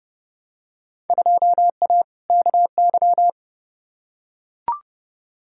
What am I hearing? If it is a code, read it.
Morse code, heard as a keyed tone, about 30 wpm, 700 Hz. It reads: 2A KY
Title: Field Day Exchanges - 30wpm